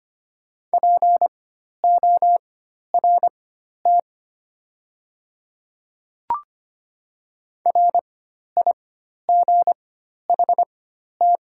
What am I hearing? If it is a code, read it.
Morse code, heard as a keyed tone, about 25 wpm, spaced as for 12 wpm, 700 Hz. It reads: PORT RIGHT
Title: 4x Character Spacing - Top 300 Words - 25wpm